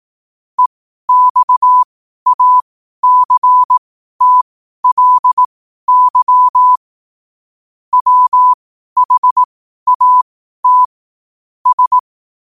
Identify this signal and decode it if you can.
Morse code, heard as a keyed tone, about 18 wpm, spaced as for 12 wpm, 1000 Hz. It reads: EXACTLY WHAT S
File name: qotdfiles / 12wpm